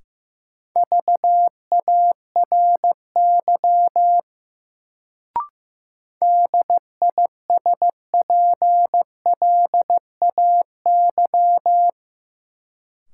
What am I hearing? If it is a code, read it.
Morse code, heard as a keyed tone, about 16 wpm, 700 Hz. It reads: VARY DISPLAY
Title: Sentences from Top 1200 Words - New Word Drill - 15wpm